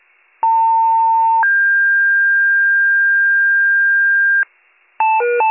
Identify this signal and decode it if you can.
Two-tone page, heard as a one-second tone, then a three-second tone, about 900 then 1600 Hz, with noise. It is heard from 0.4 to 4.4 s.